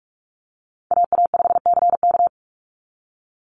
Morse code, heard as a keyed tone, about 46 wpm, 700 Hz. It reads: AA5CK
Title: NA SOTA Chaser Callsigns 45wpm